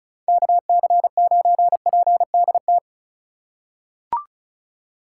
Morse code, heard as a keyed tone, about 34 wpm, 700 Hz. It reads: KC9PDT
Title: USA Call Signs - 6 Characters - 2x Character Spacing - 35wpm